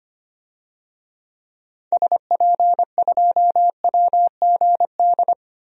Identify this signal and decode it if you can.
Morse code, heard as a keyed tone, about 25 wpm, 700 Hz. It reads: SP2WGD
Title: DXCC-100 Call Signs - 6 Characters - Repeated 2x - 25wpm